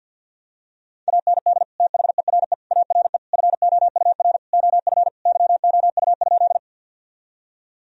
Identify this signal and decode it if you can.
Morse code, heard as a keyed tone, about 53 wpm, 700 Hz. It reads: AND THERE ARE FOUR OF YOU?
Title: Sentences from Top 1000 Words - Review Part 1 - 50wpm